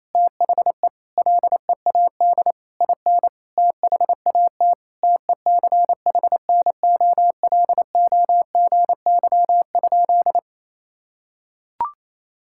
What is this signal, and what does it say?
Morse code, heard as a keyed tone, about 28 wpm, 700 Hz. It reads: THE LEAD IN THAT TECHNOLOGY?
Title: Sentences from Top 1000 Words - Review Part 5 - 28wpm